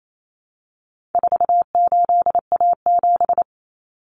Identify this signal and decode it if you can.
Morse code, heard as a keyed tone, about 28 wpm, 700 Hz. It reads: 48A7
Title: Common Cut Numbers - 4-Digit 28wpm